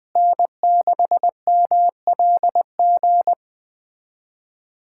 Morse code, heard as a keyed tone, about 20 wpm, 700 Hz. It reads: N6MLG
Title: USA Callsign DE Callsign - 5 Characters - 20wpm